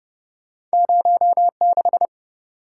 Morse code, heard as a keyed tone, about 30 wpm, 700 Hz. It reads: 0 6